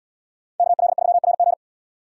Morse code, heard as a keyed tone, about 77 wpm, 690 Hz. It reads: KL3RG